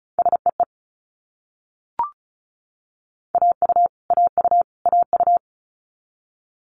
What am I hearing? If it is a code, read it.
Morse code, heard as a keyed tone, about 36 wpm, 700 Hz. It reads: SEE AU AU AU
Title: Multi-letter Phonemes Repeated 3x 35wpm